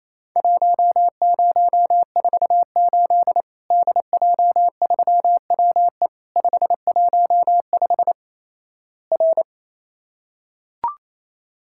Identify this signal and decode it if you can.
Morse code, heard as a keyed tone, about 27 wpm, 700 Hz. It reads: 1048 DJ3WE 515 R
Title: Worked All Europe DX Contest - QTC Exchange - 28wpm